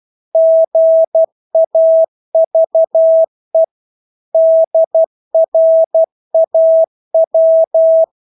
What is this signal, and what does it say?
Morse code, heard as a keyed tone, about 12 wpm, 650 Hz. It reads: GAVE DRAW
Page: Engelske ord 12 wpm